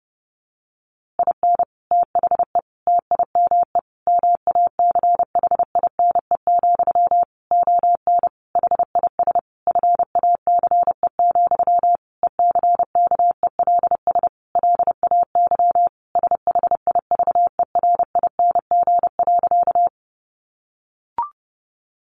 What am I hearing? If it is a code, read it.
Morse code, heard as a keyed tone, about 30 wpm, 700 Hz. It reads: IN THE TIME MACHINE, ON HIS FACE, ECKELS LAY SHIVERING.
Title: A Sound Of Thunder 30wpm